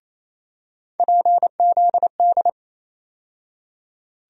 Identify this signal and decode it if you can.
Morse code, heard as a keyed tone, about 28 wpm, 700 Hz. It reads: PZD
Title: USA Call Sign Postfixes - 3 Letters 28wpm